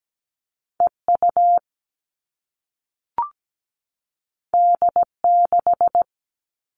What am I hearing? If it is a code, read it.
Morse code, heard as a keyed tone, about 17 wpm, 700 Hz. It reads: EU D6
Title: All Cut Numbers - 2-Digit 17wpm